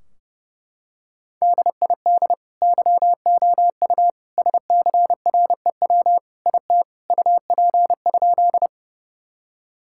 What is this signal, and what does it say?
Morse code, heard as a keyed tone, about 30 wpm, 700 Hz. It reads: DID YOU SCREW IT UP?
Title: Sentences from Top 1600 Words 30wpm